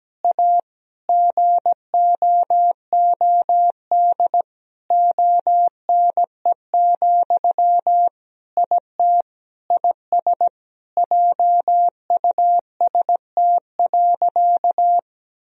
Morse code, heard as a keyed tone, about 17 wpm, 700 Hz. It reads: A GOOD ONE, IT IS JUST.
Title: Sentences from Top 1000 Words - Review Part 1 - 17wpm